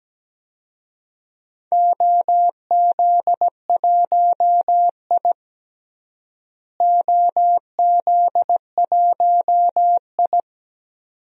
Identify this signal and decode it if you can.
Morse code, heard as a keyed tone, about 17 wpm, 700 Hz. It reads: OZ1I OZ1I
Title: DXCC-100 Call Signs - 4 Characters - Repeated 2x - 17wpm